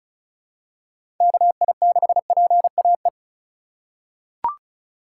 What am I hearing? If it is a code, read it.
Morse code, heard as a keyed tone, about 34 wpm, 700 Hz. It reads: KI6PAE
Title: USA Call Signs - 6 Characters - 2x Character Spacing - 35wpm